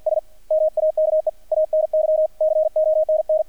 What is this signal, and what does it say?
Morse code, heard as a keyed tone, about 50 wpm, 630 Hz, with radio noise. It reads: S MAKE ANY CONN
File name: archived / CRSnet